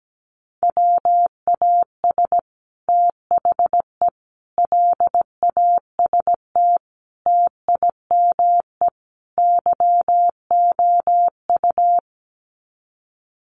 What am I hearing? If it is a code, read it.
Morse code, heard as a keyed tone, about 17 wpm, 700 Hz. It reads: WAS THE LAST TIME YOU
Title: Sets of 5 Words Encore2 17wpm